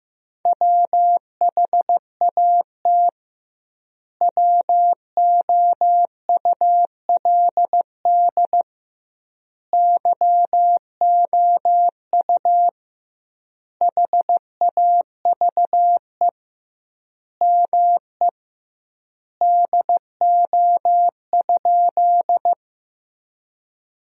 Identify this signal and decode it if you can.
Morse code, heard as a keyed tone, about 16 wpm, 700 Hz. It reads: WHAT WOULD YOU HAVE ME DO?
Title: 2x Word-Spacing - Sentences from Top 200 Words - 15wpm